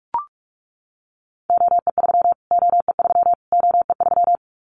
Morse code, heard as a keyed tone, about 44 wpm, 700 Hz. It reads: OE3 OE3 OE3
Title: Callsign Prefixes for Top 25 DXCC - Repeated 3x - 45wpm